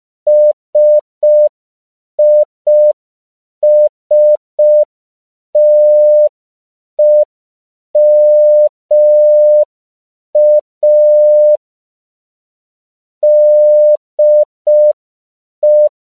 Morse code, heard as a keyed tone, about 5 wpm, 600 Hz. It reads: SISTEMA DE
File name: CW / 5PPM